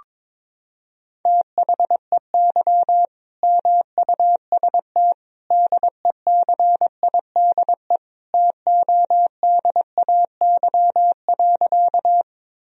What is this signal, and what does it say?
Morse code, heard as a keyed tone, about 22 wpm, 700 Hz. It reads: THEY MUST DECIDE TODAY.
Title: Sentences from Top 1000 Words - Review Part 3 - 22wpm